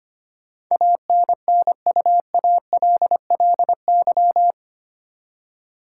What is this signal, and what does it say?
Morse code, heard as a keyed tone, about 25 wpm, 700 Hz. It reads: ANNUALLY
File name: 8 Letter Words - First 3 Letters Given - 25wpm